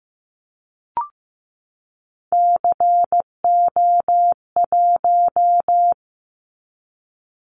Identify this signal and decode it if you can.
Morse code, heard as a keyed tone, about 15 wpm, 700 Hz. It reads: CO1
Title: Callsign Prefixes for Top 100 DXCC - 15wpm